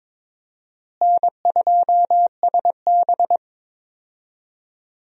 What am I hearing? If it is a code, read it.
Morse code, heard as a keyed tone, about 22 wpm, 700 Hz. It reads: N2SB
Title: USA Call Signs - 4 Characters - Repeated 2x - 22wpm